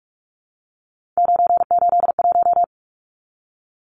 Morse code, heard as a keyed tone, about 44 wpm, 700 Hz. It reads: 981
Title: Numbers - 3-Digit 45wpm